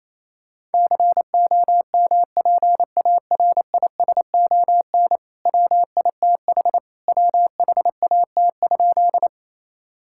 Morse code, heard as a keyed tone, about 28 wpm, 700 Hz. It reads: COMPARISON WITH WHAT?
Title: Sentences from Top 1600 Words 28wpm